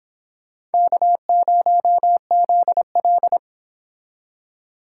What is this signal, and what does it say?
Morse code, heard as a keyed tone, about 31 wpm, 700 Hz. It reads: K0ZL